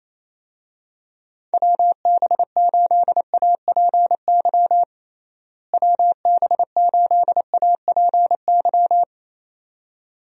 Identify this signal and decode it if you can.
Morse code, heard as a keyed tone, about 28 wpm, 700 Hz. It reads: WB8APY WB8APY
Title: USA Call Signs - 6 Characters - Repeated 2x - 28wpm